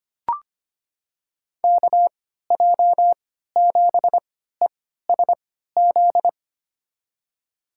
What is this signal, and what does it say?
Morse code, heard as a keyed tone, about 25 wpm, spaced as for 14 wpm, 700 Hz. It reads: KJ7ESZ
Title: USA Call Signs - 3x Character Spacing - 25wpm